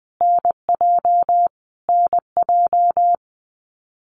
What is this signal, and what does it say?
Morse code, heard as a keyed tone, about 20 wpm, 700 Hz. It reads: NJ NJ
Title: US States Repeated 3x 20wpm